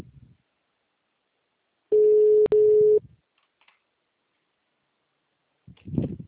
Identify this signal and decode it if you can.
Ringback tone (Europe, 425 Hz).